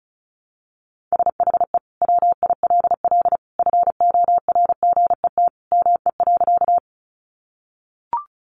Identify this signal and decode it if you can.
Morse code, heard as a keyed tone, about 34 wpm, 700 Hz. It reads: SHE WILL FORGET ME.
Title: Sentences from Top 1200 Words 35wpm